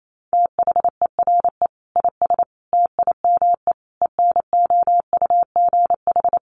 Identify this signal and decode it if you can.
Morse code, heard as a keyed tone, about 28 wpm, 700 Hz. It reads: THERE IS TIME ENOUGH
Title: Sentences from Top 1200 Words 28wpm